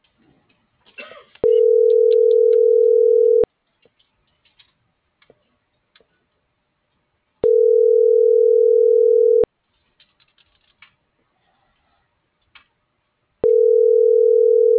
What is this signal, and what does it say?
Ringback tone (North America), 440 + 480 Hz.